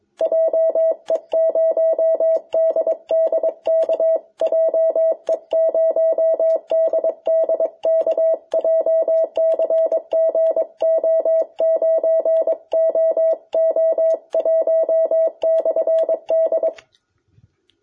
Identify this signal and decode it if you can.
Morse code, heard as a keyed tone, about 22 wpm, 650 Hz. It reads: JE0DDK JE0DDK JCG O9OO1/D
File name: jh8cbh / 2021cw